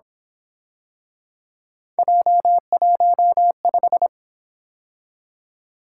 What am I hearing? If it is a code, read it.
Morse code, heard as a keyed tone, about 27 wpm, 700 Hz. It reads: J15
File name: Speed-Racing - Three Letter-Number Combinations - 22wpm